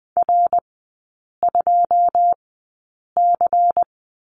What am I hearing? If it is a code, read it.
Morse code, heard as a keyed tone, about 20 wpm, spaced as for 8 wpm, 690 Hz. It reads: R2C
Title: 093 Lesson 16e - Letter c - Three Characters 20wpm